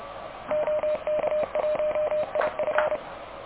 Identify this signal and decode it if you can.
Morse code, heard as a keyed tone, about 30 wpm, 610 Hz, with noise. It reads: OK1IF
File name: truSDX / KVPA ruSDXC